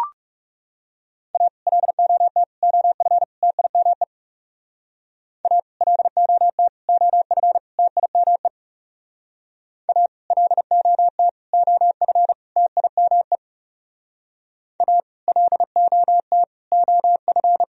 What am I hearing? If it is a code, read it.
Morse code, heard as a keyed tone, about 36 wpm, 700 Hz. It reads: A LOT OF TIME A LOT OF TIME A LOT OF TIME A LOT OF
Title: Speed-Racing - Sets of 4 Words 30wpm